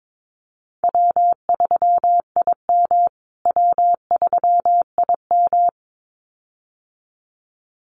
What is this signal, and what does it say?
Morse code, heard as a keyed tone, about 22 wpm, 700 Hz. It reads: W3IM W3IM
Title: USA Call Signs Repeated 2x 22wpm